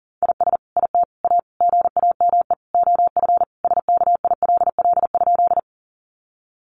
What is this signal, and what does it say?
Morse code, heard as a keyed tone, about 40 wpm, 700 Hz. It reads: IS IT A GAME OF SKILL?
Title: Sentences from Top 1200 Words 40wpm